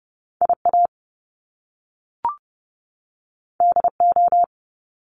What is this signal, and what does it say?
Morse code, heard as a keyed tone, about 30 wpm, 700 Hz. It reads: IA DO
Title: Top Two Letter Combinations in Words 30wpm